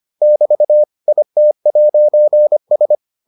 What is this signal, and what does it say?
Morse code, heard as a keyed tone, about 25 wpm, 600 Hz. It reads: <BT> IT'S